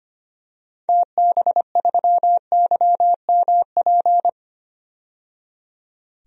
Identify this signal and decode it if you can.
Morse code, heard as a keyed tone, about 25 wpm, 700 Hz. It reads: TB3YMP
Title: DXCC-100 Call Signs - 6 Characters - 25wpm